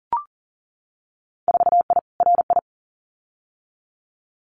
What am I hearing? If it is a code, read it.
Morse code, heard as a keyed tone, about 39 wpm, 700 Hz. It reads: 4I RI
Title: Winter Field Day Exchanges Encore 40wpm